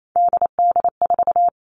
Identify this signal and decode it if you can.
Morse code, heard as a keyed tone, about 28 wpm, 700 Hz. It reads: DD4
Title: Three Challenging Letter-Number Combinations 28wpm